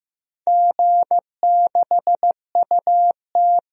Morse code, heard as a keyed tone, about 15 wpm, 700 Hz. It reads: G6UT
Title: DXCC-100 Call Signs - 4 Characters - Repeated 3x - 15wpm